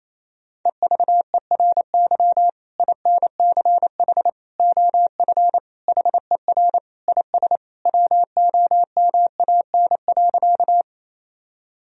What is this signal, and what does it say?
Morse code, heard as a keyed tone, about 28 wpm, 700 Hz. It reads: EVERY INCH OF HER IS WOMAN.
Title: Sentences from Top 1000 Words - Review Part 3 - 28wpm